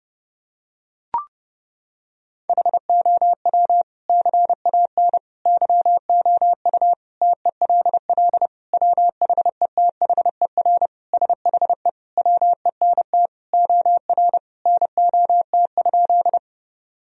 Morse code, heard as a keyed tone, about 30 wpm, 700 Hz. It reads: HOW CAN YOU TELL WHETHER SHE WENT OR NOT?
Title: Sentences from Top 1000 Words - Review Part 3 - 30wpm